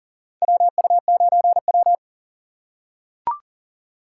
Morse code, heard as a keyed tone, about 40 wpm, 700 Hz. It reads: WU9W
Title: USA Call Signs - 4 Characters - 3x Character Spacing - 40wpm